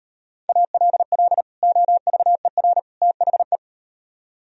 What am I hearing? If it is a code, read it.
Morse code, heard as a keyed tone, about 38 wpm, 700 Hz. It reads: ALL OVER THE